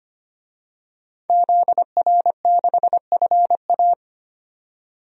Morse code, heard as a keyed tone, about 25 wpm, 700 Hz. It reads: ZR6FA
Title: DXCC-100 Call Signs - 5 Characters - 25wpm